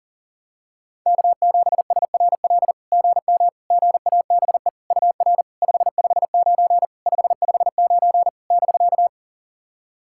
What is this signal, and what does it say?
Morse code, heard as a keyed tone, about 40 wpm, 700 Hz. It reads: K7SRL GM GABE UR 559 559 <BK>
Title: SOTA QSO Practice 40wpm